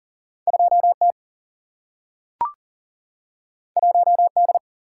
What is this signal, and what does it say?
Morse code, heard as a keyed tone, about 38 wpm, 700 Hz. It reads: 2 T 1 D
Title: Single-Letter Single-Number 40wpm